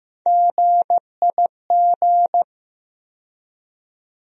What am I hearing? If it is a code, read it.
Morse code, heard as a keyed tone, about 15 wpm, 700 Hz. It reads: GIG
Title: USA Call Sign Postfixes - 3 Letters 15wpm